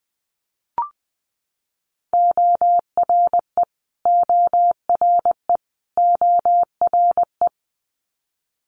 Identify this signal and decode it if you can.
Morse code, heard as a keyed tone, about 20 wpm, 700 Hz. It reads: ORE ORE ORE
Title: Multi-letter Phonemes Repeated 3x 20wpm